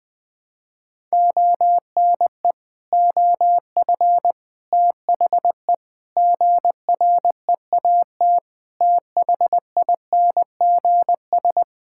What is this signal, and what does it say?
Morse code, heard as a keyed tone, about 20 wpm, 700 Hz. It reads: ONE OF THE GREAT THINGS
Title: Sentences with 6-Word Phrases - 20wpm